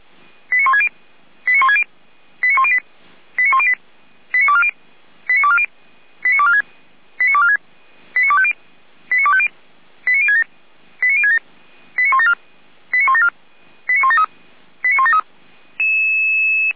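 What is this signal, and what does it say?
Five-tone selective call (ZVEI), heard as quick runs of five tones, tones of about 70 ms. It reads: 89160 89160 89108 89108 89240 89240 89246 89246 89250 89250 89067 89067 89174 89174 89173 89173
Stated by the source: Five-tone ZVEI selective call received on an SDR.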